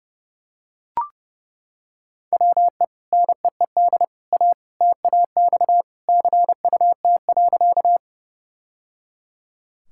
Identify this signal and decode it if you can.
Morse code, heard as a keyed tone, about 30 wpm, 700 Hz. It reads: WE NEED A TAX CUT.
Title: Sentences from Top 1000 Words - Review Part 5 - 30wpm